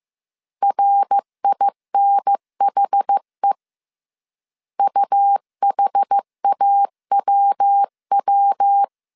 Morse code, heard as a keyed tone, about 15 wpm, 800 Hz. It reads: RINHE UHAWW
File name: CW / DET / 1 3 1500C S A 14wpm 2447